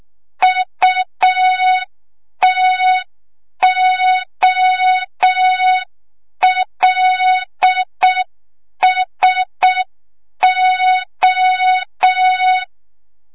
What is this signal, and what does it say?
Morse code, heard as a keyed tone, about 6 wpm, 3700 Hz. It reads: UTOLSO
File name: Morze-HA2MN / hang / 25 GYAKORLAT